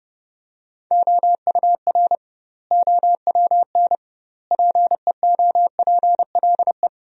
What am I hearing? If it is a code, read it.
Morse code, heard as a keyed tone, about 30 wpm, 700 Hz. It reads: OUR OWN PEOPLE
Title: 2x Word-Spacing - Sentences from Top 300 Words - 30wpm